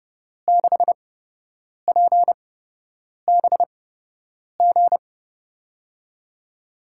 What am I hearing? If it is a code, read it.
Morse code, heard as a keyed tone, about 30 wpm, spaced as for 8 wpm, 700 Hz. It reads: 6PBG
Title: USA Call Signs - 8x Character Spacing - 30wpm